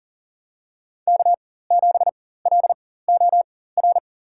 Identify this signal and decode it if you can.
Morse code, heard as a keyed tone, about 37 wpm, spaced as for 18 wpm, 700 Hz. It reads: K7LOR
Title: USA Call Signs - 5 Characters - 4x Character Spacing - 40wpm